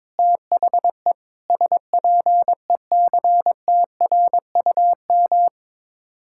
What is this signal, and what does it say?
Morse code, heard as a keyed tone, about 22 wpm, 700 Hz. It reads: THE SPECTRUM
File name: practice / 22wpm / Sets of 7 Words - 22wpm